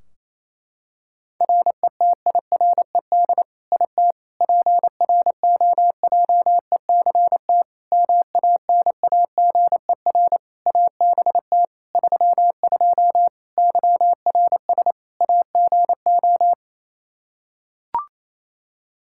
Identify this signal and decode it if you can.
Morse code, heard as a keyed tone, about 28 wpm, 700 Hz. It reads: RETIRED IT PROJECT MANAGER ABT 32 YRS AGO
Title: Practice QSO - Exchange 3 - More Info - 28wpm